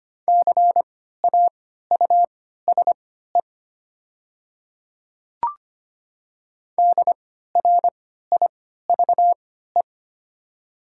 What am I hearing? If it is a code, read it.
Morse code, heard as a keyed tone, about 25 wpm, spaced as for 14 wpm, 690 Hz. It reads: CAUSE DRIVE
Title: 3x Character Spacing - Top 500 Words - 25wpm